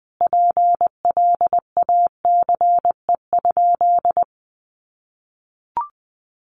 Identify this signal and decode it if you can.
Morse code, heard as a keyed tone, about 19 wpm, 700 Hz. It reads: PLACE?
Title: Sentences from Top 200 Words 20wpm